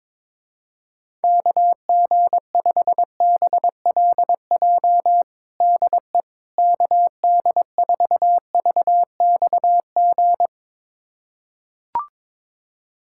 Morse code, heard as a keyed tone, about 22 wpm, 700 Hz. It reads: KG5BLJ DE KD4VXG
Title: USA Callsign DE Callsign 22wpm